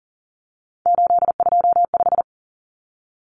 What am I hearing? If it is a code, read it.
Morse code, heard as a keyed tone, about 40 wpm, 700 Hz. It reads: 825
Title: Common Cut Numbers - 3-Digit 40wpm